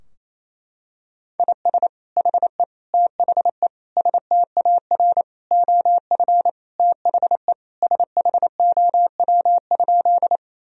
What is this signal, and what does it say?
Morse code, heard as a keyed tone, about 28 wpm, 700 Hz. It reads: IS HE THE STAR OF THE SHOW?
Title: Sentences from Top 1000 Words - Review Part 2 - 28wpm